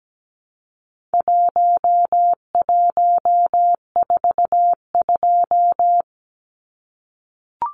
Morse code, heard as a keyed tone, about 17 wpm, 700 Hz. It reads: 1142
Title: Numbers - Contest Style - 4-Digit 17wpm